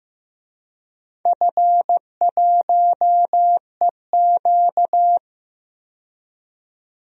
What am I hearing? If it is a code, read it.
Morse code, heard as a keyed tone, about 15 wpm, 700 Hz. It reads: F1EQ
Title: DXCC-100 Call Signs - 4 Characters - 15wpm